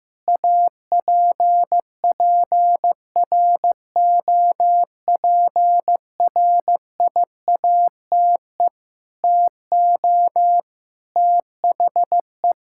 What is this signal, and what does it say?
Morse code, heard as a keyed tone, about 16 wpm, 700 Hz. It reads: APPROPRIATE TO THE
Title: Sentences from Top 1000 Words - Review Part 5 - 15wpm